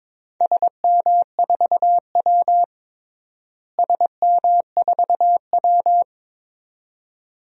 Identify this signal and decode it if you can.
Morse code, heard as a keyed tone, about 22 wpm, 700 Hz. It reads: SM4W SM4W
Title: DXCC-100 Call Signs - 4 Characters - Repeated 2x - 22wpm